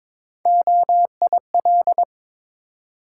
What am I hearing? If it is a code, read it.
Morse code, heard as a keyed tone, about 22 wpm, 700 Hz. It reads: OIL
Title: Sentences from Top 1000 Words - Review Part 4 - New Word Drill - 22wpm